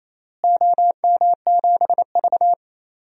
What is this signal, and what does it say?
Morse code, heard as a keyed tone, about 28 wpm, 700 Hz. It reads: OM7V
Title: DXCC-100 Call Signs - 4 Characters - Repeated 3x - 28wpm